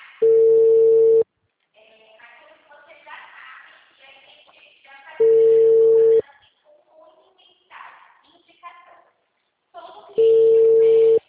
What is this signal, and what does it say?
Ringback tone (China, 450 Hz), with speech.